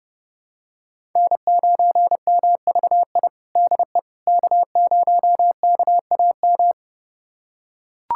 Morse code, heard as a keyed tone, about 30 wpm, 700 Hz. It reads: N9MVI DE K0KAM
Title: USA Callsign DE Callsign - 5 Characters - 30wpm